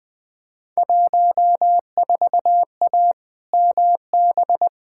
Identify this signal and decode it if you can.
Morse code, heard as a keyed tone, about 20 wpm, 700 Hz. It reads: 14A MB
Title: Field Day Exchanges Encore - 20wpm